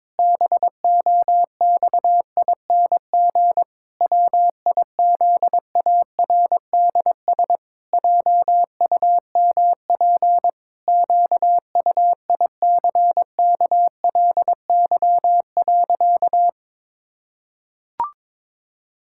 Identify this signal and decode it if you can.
Morse code, heard as a keyed tone, about 22 wpm, 700 Hz. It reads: BOXING WIZARDS JUMP QUICKLY.
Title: Pangram Challenge v2 - 22wpm